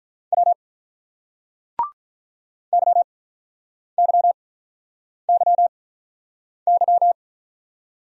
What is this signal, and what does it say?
Morse code, heard as a keyed tone, about 46 wpm, spaced as for 9 wpm, 700 Hz. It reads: W YYYY